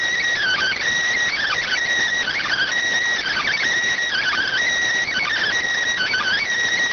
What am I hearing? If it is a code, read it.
Emergency Alert System (SAME) tones (header bursts), heard as harsh buzzing bursts. It is heard from 0.4 to 6.4 s.